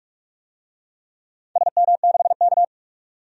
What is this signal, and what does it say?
Morse code, heard as a keyed tone, about 46 wpm, 700 Hz. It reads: SM6K
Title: DXCC-100 Call Signs - 4 Characters - 45wpm